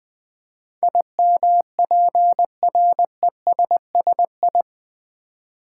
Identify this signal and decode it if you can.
Morse code, heard as a keyed tone, about 20 wpm, 700 Hz. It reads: IMPRESSI
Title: Build A Word - 4 Letters At A Time - 10-Letter Words - 20wpm